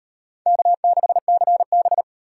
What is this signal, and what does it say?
Morse code, heard as a keyed tone, about 38 wpm, 700 Hz. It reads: K6CB